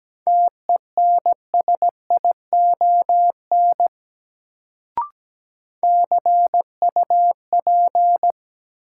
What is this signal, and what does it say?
Morse code, heard as a keyed tone, about 17 wpm, 700 Hz. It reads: TENSION CUP
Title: Sentences from Top 1600 Words - New Word Drill - 17wpm